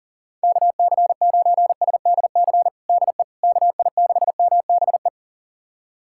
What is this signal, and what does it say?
Morse code, heard as a keyed tone, about 40 wpm, 700 Hz. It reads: KC9SDC DE KI6MBE
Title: USA Callsign DE Callsign - 6 Characters - 40wpm